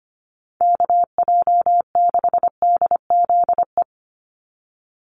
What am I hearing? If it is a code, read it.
Morse code, heard as a keyed tone, about 25 wpm, 700 Hz. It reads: KJ6DZE
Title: USA Call Signs 25wpm